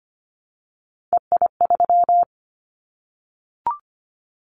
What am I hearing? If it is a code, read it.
Morse code, heard as a keyed tone, about 25 wpm, 700 Hz. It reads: EI3
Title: Callsign Prefixes for Top 50 DXCC - 25wpm